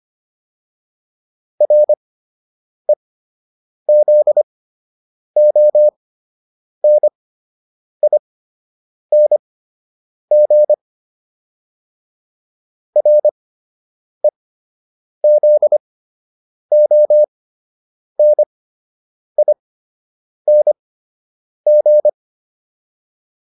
Morse code, heard as a keyed tone, about 25 wpm, spaced as for 8 wpm, 600 Hz. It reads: REZONING REZONING